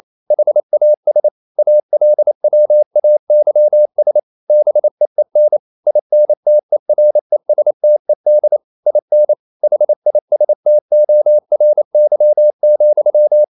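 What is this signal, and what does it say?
Morse code, heard as a keyed tone, about 28 wpm, 600 Hz. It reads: HAS ALWAYS BEEN INTERESTED IN HISTORY,
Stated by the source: Morse code practice at 28 wpm, text of Ken Follett's novel Never in English, from CW med Gnister.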